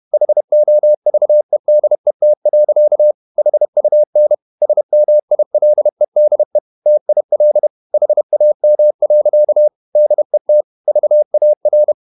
Morse code, heard as a keyed tone, about 31 wpm, 600 Hz. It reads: HOVEDET. HUN SMILEDE TIL HAM. DET VAR
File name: Ferie QRQ 31wpm